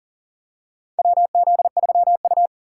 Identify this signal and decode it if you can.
Morse code, heard as a keyed tone, about 40 wpm, 700 Hz. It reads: WZ3U
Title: USA Call Signs - 2x Character Spacing - 40wpm